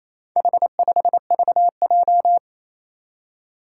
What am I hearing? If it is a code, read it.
Morse code, heard as a keyed tone, about 28 wpm, 700 Hz. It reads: H5VJ
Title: Four Challenging Letter-Number Combinations 28wpm